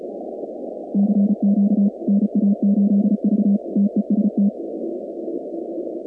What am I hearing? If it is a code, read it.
Morse code, heard as a keyed tone, about 34 wpm, 200 Hz. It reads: CQ NA8V TEST
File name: contests / 2019-cw-wpx-kt5j-k5pi- -2330 0-46